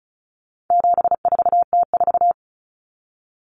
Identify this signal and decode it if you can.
Morse code, heard as a keyed tone, about 34 wpm, 700 Hz. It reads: 74T4
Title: Common Cut Numbers - 4-Digit 35wpm